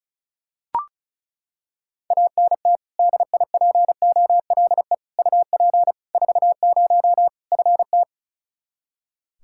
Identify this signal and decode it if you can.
Morse code, heard as a keyed tone, about 34 wpm, 700 Hz. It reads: ANT DIPOLE UP 40 FT
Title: Practice QSO - Exchange 2 - Info - 35wpm